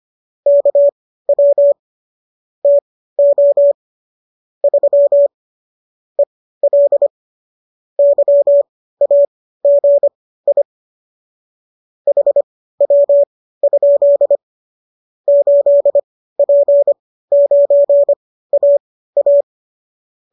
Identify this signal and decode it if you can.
Morse code, heard as a keyed tone, about 25 wpm, spaced as for 15 wpm, 570 Hz. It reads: KW TO 3 EL YAGI HW? 8P9AA